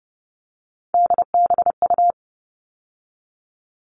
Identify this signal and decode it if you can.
Morse code, heard as a keyed tone, about 30 wpm, 700 Hz. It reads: DBU
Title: All Cut Numbers - 3-Digit 30wpm